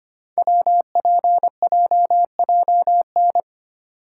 Morse code, heard as a keyed tone, about 25 wpm, 700 Hz. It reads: WPJJN
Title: Code Groups - 5 Letters - 25wpm